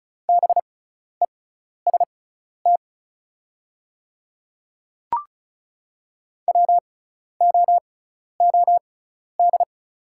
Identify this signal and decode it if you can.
Morse code, heard as a keyed tone, about 34 wpm, spaced as for 12 wpm, 700 Hz. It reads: BEST WOOD
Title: 6x Character Spacing - Top 500 Words - 35wpm